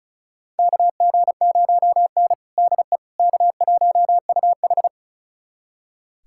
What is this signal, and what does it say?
Morse code, heard as a keyed tone, about 36 wpm, 700 Hz. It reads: KG0N DE K1UH
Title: USA Callsign DE Callsign - 4 Characters - 35wpm